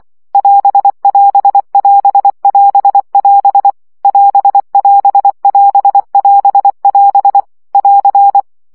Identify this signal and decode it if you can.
Morse code, heard as a keyed tone, about 24 wpm, 800 Hz. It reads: <AS><AS><AS><AS><AS> <AS><AS><AS><AS><AS> <AR>
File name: CW gyakorlas as 20 baud